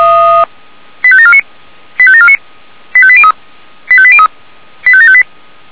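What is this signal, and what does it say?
Five-tone selective call (ZVEI), heard as quick runs of five tones, tones of about 70 ms. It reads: 85749 85749 85803 85803 85758
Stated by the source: ZVEI five-tone call received by SDR.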